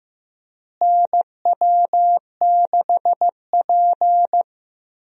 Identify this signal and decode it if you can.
Morse code, heard as a keyed tone, about 15 wpm, 700 Hz. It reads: NW6P
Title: USA Call Signs - 4 Characters - Repeated 3x - 15wpm